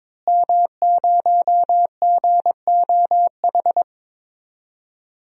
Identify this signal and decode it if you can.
Morse code, heard as a keyed tone, about 22 wpm, 700 Hz. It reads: M0GOH
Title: DXCC-100 Call Signs - 5 Characters - Repeated 2x - 22wpm